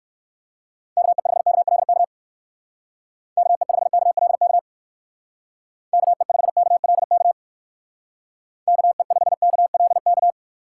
Morse code, heard as a keyed tone, about 58 wpm, 700 Hz. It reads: KE5KLK KE5KLK KE5KLK KE5KLK